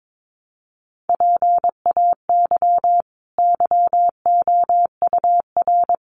Morse code, heard as a keyed tone, about 22 wpm, 700 Hz. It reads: PAY YOUR
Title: Sentences from Top 900 Words 22wpm